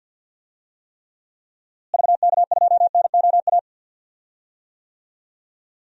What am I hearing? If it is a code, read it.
Morse code, heard as a keyed tone, about 46 wpm, 700 Hz. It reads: VK2NOA
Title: DXCC-100 Call Signs - 6 Characters - Repeated 2x - 50wpm